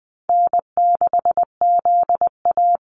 Morse code, heard as a keyed tone, about 20 wpm, 700 Hz. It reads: N6ZA
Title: NA SOTA Activator Callsigns 20wpm